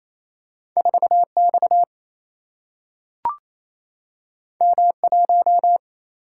Morse code, heard as a keyed tone, about 28 wpm, 700 Hz. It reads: 4X M1
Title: Single-Letter Single-Number 28wpm